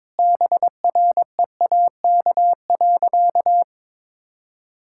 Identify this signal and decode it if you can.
Morse code, heard as a keyed tone, about 22 wpm, 700 Hz. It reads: BREAK.
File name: practice / 22wpm / Sentences with 4-Word Phrases - 22wpm